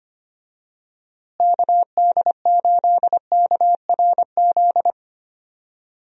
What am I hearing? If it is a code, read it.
Morse code, heard as a keyed tone, about 25 wpm, 700 Hz. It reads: KD8KRZ
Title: USA Call Signs - 6 Characters - Repeated 2x - 25wpm